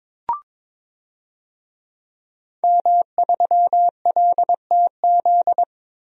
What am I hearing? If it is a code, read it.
Morse code, heard as a keyed tone, about 22 wpm, 700 Hz. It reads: M3LTZ
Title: DXCC-100 Call Signs - 5 Characters - Repeated 3x - 22wpm